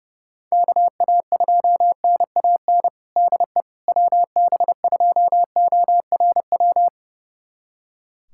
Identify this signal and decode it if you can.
Morse code, heard as a keyed tone, about 30 wpm, 700 Hz. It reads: KA2NAN DE WB2ORW
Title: USA Callsign DE Callsign 30wpm